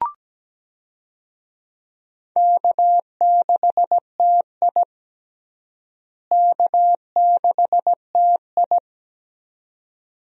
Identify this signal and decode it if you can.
Morse code, heard as a keyed tone, about 17 wpm, 700 Hz. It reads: K6TI K6TI
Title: USA Call Signs - 4 Characters - Repeated 2x - 17wpm